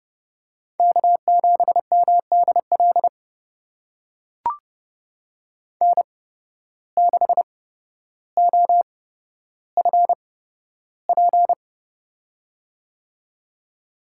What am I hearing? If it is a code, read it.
Morse code, heard as a keyed tone, about 30 wpm, spaced as for 13 wpm, 700 Hz. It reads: K7MDL N6OFP
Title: USA Call Signs - 8x Character Spacing - 30wpm